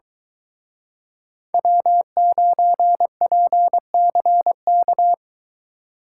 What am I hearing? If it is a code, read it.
Morse code, heard as a keyed tone, about 22 wpm, 700 Hz. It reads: W9PCK